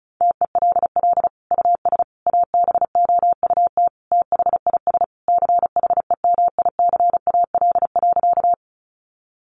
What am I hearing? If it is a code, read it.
Morse code, heard as a keyed tone, about 34 wpm, 700 Hz. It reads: TELL US ABOUT THIS CHEMICAL.
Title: Sentences from Top 1200 Words 35wpm